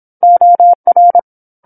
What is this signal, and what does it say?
Morse code, heard as a keyed tone, about 27 wpm, 700 Hz. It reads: OR
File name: 08-Chapter-08 26 WPM